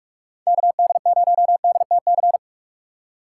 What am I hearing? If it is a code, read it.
Morse code, heard as a keyed tone, about 42 wpm, 700 Hz. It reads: KD0DTC
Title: USA Call Signs - 6 Characters - 4x Character Spacing - 45wpm